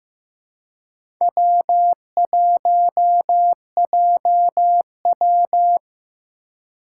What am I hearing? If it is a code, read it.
Morse code, heard as a keyed tone, about 15 wpm, 700 Hz. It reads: W1JW
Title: USA Call Signs - 3x Character Spacing - 15wpm